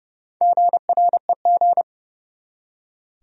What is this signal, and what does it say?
Morse code, heard as a keyed tone, about 31 wpm, 700 Hz. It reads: GREG
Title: USA Common Names - Max 5 Letters - 30wpm